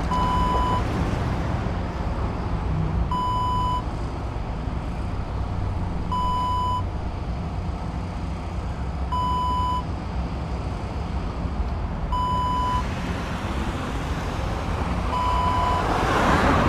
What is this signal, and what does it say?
There is no code.